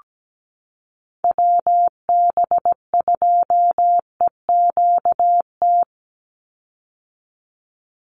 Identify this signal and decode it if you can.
Morse code, heard as a keyed tone, about 17 wpm, 700 Hz. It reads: WB2EQT
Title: USA Call Signs 17wpm